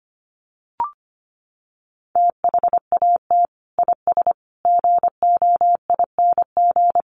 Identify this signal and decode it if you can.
Morse code, heard as a keyed tone, about 25 wpm, 700 Hz. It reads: THAT IS GOING
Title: Sets of 5 Words Encore2 25wpm